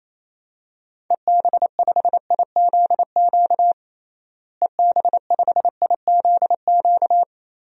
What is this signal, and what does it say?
Morse code, heard as a keyed tone, about 28 wpm, 700 Hz. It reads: EB5IZQ EB5IZQ
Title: DXCC-100 Call Signs - 6 Characters - Repeated 3x - 28wpm